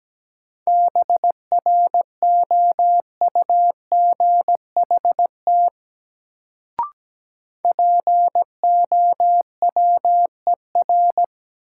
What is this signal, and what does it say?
Morse code, heard as a keyed tone, about 17 wpm, 700 Hz. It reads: BROUGHT POWER
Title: Sentences from Top 500 Words - New Word Drill - 17wpm